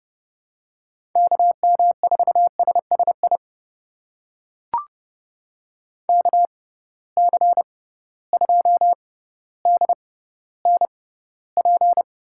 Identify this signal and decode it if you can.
Morse code, heard as a keyed tone, about 30 wpm, spaced as for 15 wpm, 700 Hz. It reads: KM4SSI KC2DNP
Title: USA Call Signs - 6 Characters - 6x Character Spacing - 30wpm